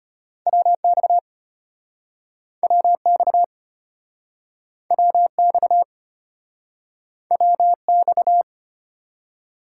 Morse code, heard as a keyed tone, about 36 wpm, 700 Hz. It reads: WX WX WX WX